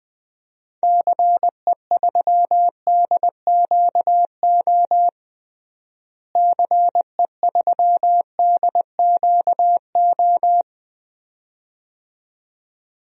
Morse code, heard as a keyed tone, about 20 wpm, 700 Hz. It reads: CE3DQO CE3DQO
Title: DXCC-100 Call Signs - 6 Characters - Repeated 2x - 20wpm